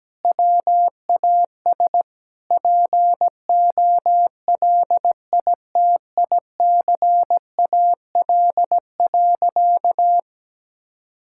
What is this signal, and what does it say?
Morse code, heard as a keyed tone, about 17 wpm, 700 Hz. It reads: WAS POLITICAL.
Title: Sentences from Top 1000 Words - Review Part 5 - 17wpm